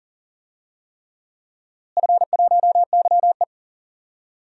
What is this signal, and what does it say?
Morse code, heard as a keyed tone, about 40 wpm, 700 Hz. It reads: F1YE
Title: DXCC-100 Call Signs - 4 Characters - Repeated 3x - 40wpm